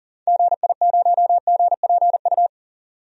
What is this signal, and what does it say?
Morse code, heard as a keyed tone, about 40 wpm, 700 Hz. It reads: GI0GPU
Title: DXCC-100 Call Signs - 6 Characters - 40wpm